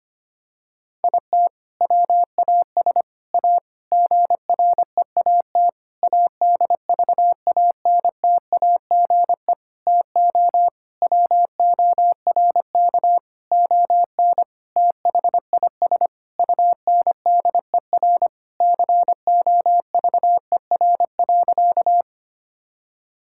Morse code, heard as a keyed tone, about 25 wpm, 700 Hz. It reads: IT WAS A GREAT ADVANTAGE TO WORK ON THIS UNDER COVER.
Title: Sentences from Top 1000 Words - Review Part 6 - 25wpm